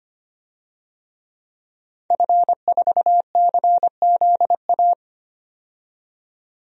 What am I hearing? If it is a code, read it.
Morse code, heard as a keyed tone, about 25 wpm, 690 Hz. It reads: F4CZA
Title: DXCC-100 Call Signs - 5 Characters - 25wpm